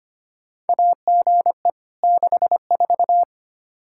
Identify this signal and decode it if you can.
Morse code, heard as a keyed tone, about 25 wpm, 700 Hz. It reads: AGE 64
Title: Practice QSO - Exchange 3 - More Info - 25wpm